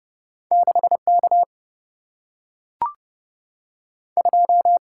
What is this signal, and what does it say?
Morse code, heard as a keyed tone, about 30 wpm, 700 Hz. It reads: 6 K 2
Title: Single-Letter Single-Number 30wpm